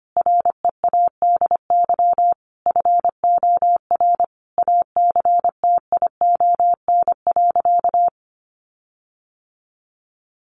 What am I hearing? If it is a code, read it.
Morse code, heard as a keyed tone, about 25 wpm, 700 Hz. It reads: READY FOR ACTION.
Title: Sentences from Top 1300 Words 25wpm